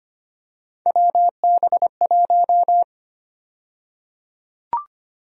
Morse code, heard as a keyed tone, about 25 wpm, 700 Hz. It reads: WB1
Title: USA Call Sign Prefixes - 25wpm